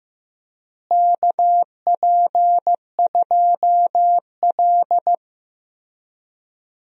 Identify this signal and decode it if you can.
Morse code, heard as a keyed tone, about 16 wpm, 700 Hz. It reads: KP2L
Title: DXCC-100 Call Signs - 4 Characters - Repeated 3x - 15wpm